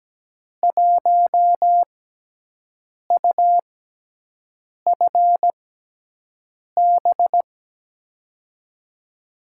Morse code, heard as a keyed tone, about 17 wpm, spaced as for 6 wpm, 700 Hz. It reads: 1UFB
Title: USA Call Signs - 5 Characters - 6x Character Spacing - 17wpm